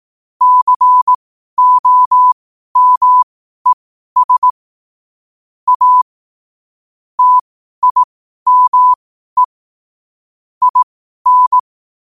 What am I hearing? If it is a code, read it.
Morse code, heard as a keyed tone, about 18 wpm, spaced as for 12 wpm, 1000 Hz. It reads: COMES A TIME IN
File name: qotdfiles / 12wpm